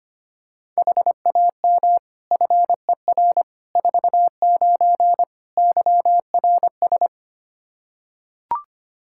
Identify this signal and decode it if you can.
Morse code, heard as a keyed tone, about 25 wpm, 700 Hz. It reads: HAM FER 49 YRS
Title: K1USN SST QSO Practice 25wpm